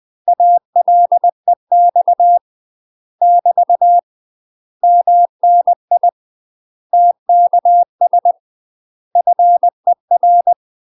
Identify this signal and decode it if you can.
Morse code, heard as a keyed tone, about 20 wpm, 700 Hz. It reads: ALEX <BT> MNI TKS FER
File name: UK MASTER QSOs TRAINING/20WPM